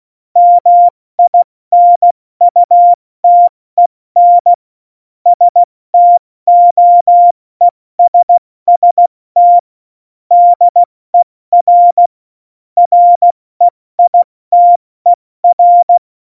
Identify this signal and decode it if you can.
Morse code, heard as a keyed tone, about 16 wpm, 700 Hz. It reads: MINUTEN STOESST DER REITER
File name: Morsekurs / Witze / Witz 11 August 2020, 70 F80